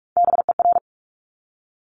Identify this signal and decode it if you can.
Morse code, heard as a keyed tone, about 46 wpm, 690 Hz. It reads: BER
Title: Top Three Letter Combinations in Words 45wpm